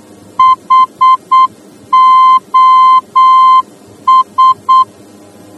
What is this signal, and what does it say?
Morse code, heard as a keyed tone, about 8 wpm, 1040 Hz. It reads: HOS
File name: germany / mx / HOS-D